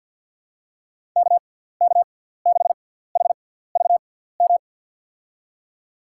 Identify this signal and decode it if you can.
Morse code, heard as a keyed tone, about 52 wpm, spaced as for 17 wpm, 700 Hz. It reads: KK6HVM
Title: USA Call Signs - 6x Character Spacing - 50wpm